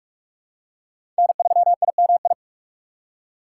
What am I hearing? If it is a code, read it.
Morse code, heard as a keyed tone, about 46 wpm, 700 Hz. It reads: N3IMI
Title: USA Call Signs - 4x Character Spacing - 45wpm